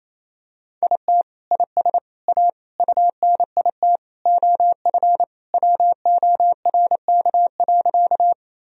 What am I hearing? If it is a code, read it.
Morse code, heard as a keyed tone, about 28 wpm, 700 Hz. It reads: IT IS A UNIT OF WORK.
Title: Sentences from Top 1000 Words - Review Part 2 - 28wpm